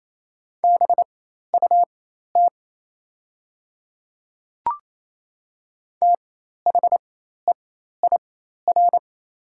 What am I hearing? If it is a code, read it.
Morse code, heard as a keyed tone, about 28 wpm, spaced as for 13 wpm, 700 Hz. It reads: BUT THEIR
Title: 4x Character Spacing - Top 100 Words - 28wpm